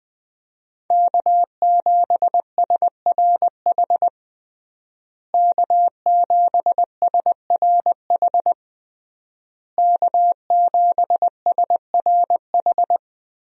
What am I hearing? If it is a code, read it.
Morse code, heard as a keyed tone, about 20 wpm, 700 Hz. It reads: K7SRH K7SRH K7SRH
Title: USA Call Signs - 5 Characters - Repeated 3x - 20wpm